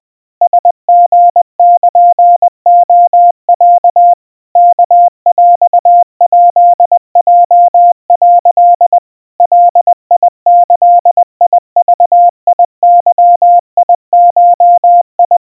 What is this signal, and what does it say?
Morse code (Japanese Wabun code), heard as a keyed tone, about 20 wpm, 700 Hz. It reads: ラリルレロ ワヰヱヲ」 ガギグゲゴ
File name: kawa / Morse sample / aiueo 70 cpm 700 Hz Crispy